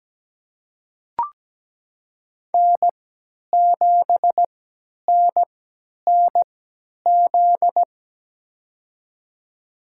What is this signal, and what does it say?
Morse code, heard as a keyed tone, about 17 wpm, spaced as for 10 wpm, 700 Hz. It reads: N7NNZ
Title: USA Call Signs - 3x Character Spacing - 17wpm